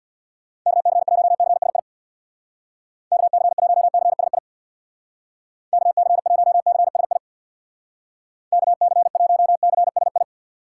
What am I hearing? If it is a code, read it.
Morse code, heard as a keyed tone, about 62 wpm, 700 Hz. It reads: KC1CSI KC1CSI KC1CSI KC1CSI
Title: Speed Racing - USA 2x3 Call Signs 50wpm